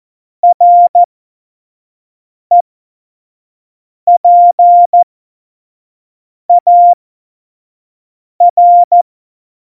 Morse code, heard as a keyed tone, about 14 wpm, spaced as for 5 wpm, 700 Hz. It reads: REPAR
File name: Morsekurs / Witze / Witz 15. Januar 2019, 25 F70